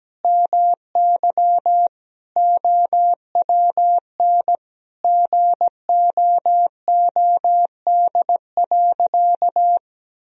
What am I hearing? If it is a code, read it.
Morse code, heard as a keyed tone, about 17 wpm, 700 Hz. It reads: MY OWN GOOD.
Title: Sentences from Top 1000 Words - Review Part 4 - 17wpm